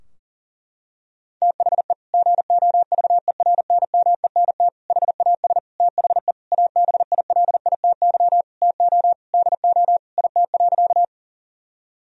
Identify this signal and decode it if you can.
Morse code, heard as a keyed tone, about 40 wpm, 700 Hz. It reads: THE GOVERNMENT HAS THE ABILITY TO DO IT.
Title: Sentences from Top 1000 Words - Review Part 3 - 40wpm